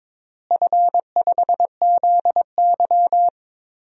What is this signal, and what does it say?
Morse code, heard as a keyed tone, about 22 wpm, 700 Hz. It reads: F5ZY
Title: DXCC-100 Call Signs - 4 Characters - Repeated 2x - 22wpm